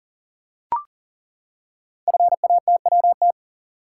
Morse code, heard as a keyed tone, about 41 wpm, 700 Hz. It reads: FATWT
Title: Code Groups - 5 Letters - 40wpm